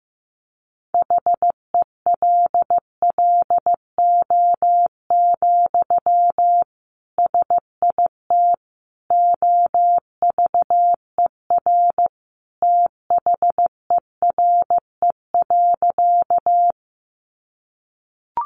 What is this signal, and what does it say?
Morse code, heard as a keyed tone, about 15 wpm, 700 Hz. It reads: HELLO, SIT OVER THERE.
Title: Sentences from Top 1100 Words 15wpm